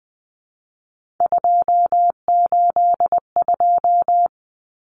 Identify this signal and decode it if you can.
Morse code, heard as a keyed tone, about 20 wpm, 700 Hz. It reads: 282
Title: Numbers - 3-Digit 20wpm